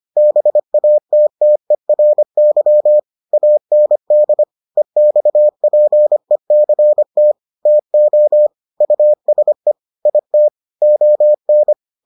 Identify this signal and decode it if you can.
Morse code, heard as a keyed tone, about 25 wpm, 600 Hz. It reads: BATTERY AND EXPECT TO USE IT ON